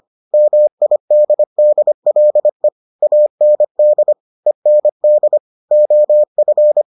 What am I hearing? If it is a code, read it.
Morse code, heard as a keyed tone, about 25 wpm, 600 Hz. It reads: MIDDLE AND END OF